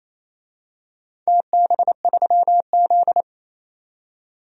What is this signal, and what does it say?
Morse code, heard as a keyed tone, about 28 wpm, 700 Hz. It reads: TB3Z
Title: DXCC-100 Call Signs - 4 Characters - 28wpm